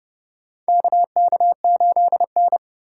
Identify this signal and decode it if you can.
Morse code, heard as a keyed tone, about 31 wpm, 700 Hz. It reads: KK8N
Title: USA Call Signs - 4 Characters - 4x Character Spacing - 30wpm